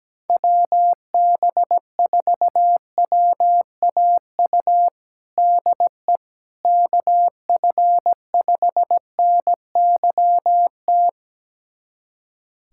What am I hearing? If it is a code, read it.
Morse code, heard as a keyed tone, about 17 wpm, 700 Hz. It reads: WB4WAU DE KF5NYT
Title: USA Callsign DE Callsign - 6 Characters - 17wpm